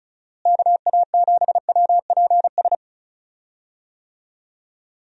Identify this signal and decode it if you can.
Morse code, heard as a keyed tone, about 36 wpm, 700 Hz. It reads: KA7WPS
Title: USA Call Signs - 6 Characters - Repeated 2x - 35wpm